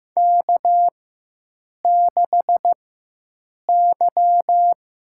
Morse code, heard as a keyed tone, about 15 wpm, spaced as for 7 wpm, 700 Hz. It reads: K6Y
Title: USA Call Signs - 4x Character Spacing - 15wpm